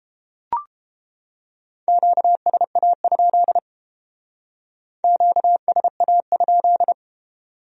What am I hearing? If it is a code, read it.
Morse code, heard as a keyed tone, about 29 wpm, 700 Hz. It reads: QSA? QSA?